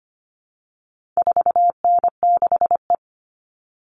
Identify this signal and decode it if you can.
Morse code, heard as a keyed tone, about 25 wpm, 700 Hz. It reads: 4N6E
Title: All Cut Numbers - 4-Digit 25wpm